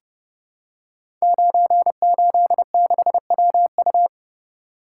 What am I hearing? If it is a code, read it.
Morse code, heard as a keyed tone, about 30 wpm, 700 Hz. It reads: 986WU
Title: Code Groups - 5 Letter:Numbers - 30wpm